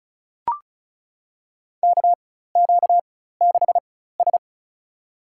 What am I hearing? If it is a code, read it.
Morse code, heard as a keyed tone, about 34 wpm, spaced as for 16 wpm, 700 Hz. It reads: KQ6S
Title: USA Call Signs - 4 Characters - 4x Character Spacing - 35wpm